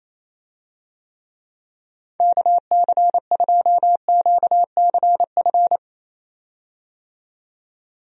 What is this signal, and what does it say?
Morse code, heard as a keyed tone, about 27 wpm, 700 Hz. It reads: KC2QCF
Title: USA Call Signs - 6 Characters - 28wpm